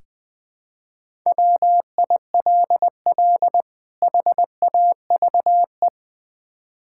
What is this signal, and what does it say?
Morse code, heard as a keyed tone, about 20 wpm, 700 Hz. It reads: WILL HAVE
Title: Build A Phrase - 3-Word Phrases - 20wpm